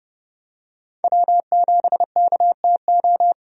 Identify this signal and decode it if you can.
Morse code, heard as a keyed tone, about 31 wpm, 700 Hz. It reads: W7KTO
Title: USA Call Signs - 5 Characters - 30wpm